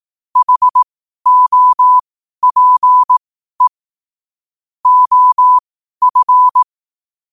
Morse code, heard as a keyed tone, about 18 wpm, spaced as for 12 wpm, 1000 Hz. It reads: HOPE OF
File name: qotdfiles / 12wpm